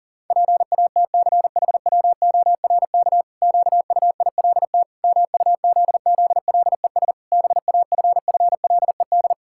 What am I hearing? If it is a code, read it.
Morse code, heard as a keyed tone, about 40 wpm, 700 Hz. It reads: PATCHWORK QUILT MUZZLES BAFFLED
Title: Pangram Challenge v2 - 40wpm